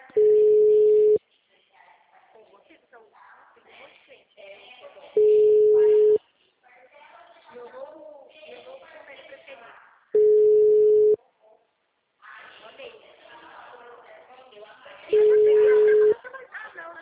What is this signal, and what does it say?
Ringback tone (Europe, 425 Hz), with speech.
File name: out- -2029- - - .38939